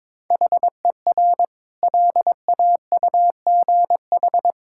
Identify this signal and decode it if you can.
Morse code, heard as a keyed tone, about 22 wpm, 700 Hz. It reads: HER LAUGH
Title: Sentences from Top 1000 Words - Review Part 5 - 22wpm